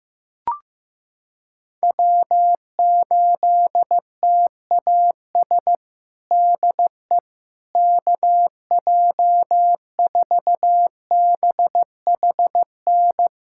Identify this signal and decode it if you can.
Morse code, heard as a keyed tone, about 15 wpm, 700 Hz. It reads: W8TAS DE KJ4BHN
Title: USA Callsign DE Callsign 15wpm